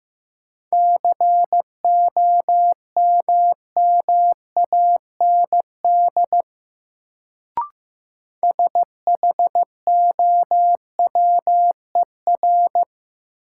Morse code, heard as a keyed tone, about 15 wpm, 700 Hz. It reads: COMMAND SHOWER
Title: Sentences from Top 1600 Words - New Word Drill Alt - 15wpm